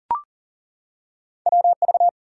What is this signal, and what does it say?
Morse code, heard as a keyed tone, about 39 wpm, 700 Hz. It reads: WV
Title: US States and Canadian Provinces 40wpm